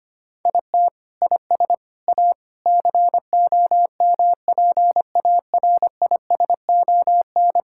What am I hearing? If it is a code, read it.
Morse code, heard as a keyed tone, about 25 wpm, 700 Hz. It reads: IT IS A COMPARISON
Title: Sentences from Top 1600 Words 25wpm